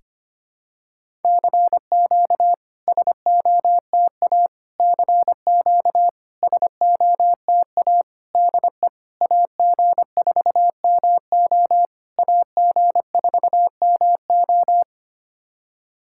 Morse code, heard as a keyed tone, about 25 wpm, 700 Hz. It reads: CQ SOTA CQ SOTA DE AG4MO AG4MO
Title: SOTA QSO Practice 25wpm